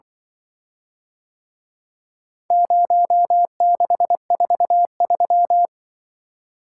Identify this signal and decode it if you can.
Morse code, heard as a keyed tone, about 24 wpm, 700 Hz. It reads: 0643
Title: Speed-Racing - Four Letter-Number Combinations - 20wpm